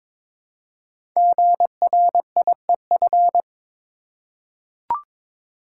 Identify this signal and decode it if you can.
Morse code, heard as a keyed tone, about 21 wpm, 690 Hz. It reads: GRIEF
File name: 5 Letter Words - First 3 Letters Given - 22wpm